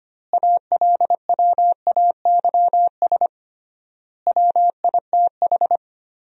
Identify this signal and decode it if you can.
Morse code, heard as a keyed tone, about 25 wpm, 700 Hz. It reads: ALWAYS WITH
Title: 3x Word-Spacing - Sentences from Top 400 Words - 25wpm